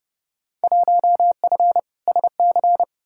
Morse code, heard as a keyed tone, about 32 wpm, 700 Hz. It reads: 1F SC
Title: Field Day Exchanges Encore - 30wpm